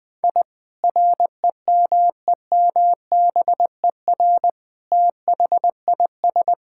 Morse code, heard as a keyed tone, about 20 wpm, 700 Hz. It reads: I REMEMBER THIS
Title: Sentences from Top 1000 Words - Review Part 5 - 20wpm